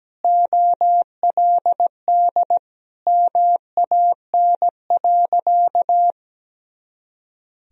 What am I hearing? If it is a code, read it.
Morse code, heard as a keyed tone, about 17 wpm, 700 Hz. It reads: OLD MAN.
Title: Sentences from Top 200 Words and Abbreviations 17wpm